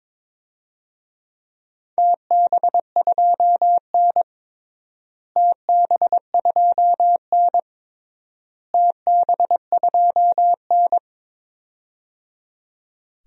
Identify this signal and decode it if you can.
Morse code, heard as a keyed tone, about 22 wpm, 700 Hz. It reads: TB2N TB2N TB2N
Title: DXCC-100 Call Signs - 4 Characters - Repeated 3x - 22wpm